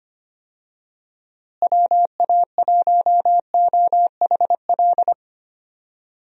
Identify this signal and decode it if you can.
Morse code, heard as a keyed tone, about 25 wpm, 700 Hz. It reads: WA1OHL
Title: USA Call Signs - 6 Characters - Repeated 2x - 25wpm